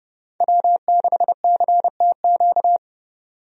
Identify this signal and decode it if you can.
Morse code, heard as a keyed tone, about 30 wpm, 700 Hz. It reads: W6CTQ
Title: USA Call Signs - 5 Characters - 30wpm